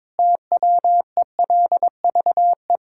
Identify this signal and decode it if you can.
Morse code, heard as a keyed tone, about 22 wpm, 700 Hz. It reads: TWELVE
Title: Pangram Challenge v2 - 22wpm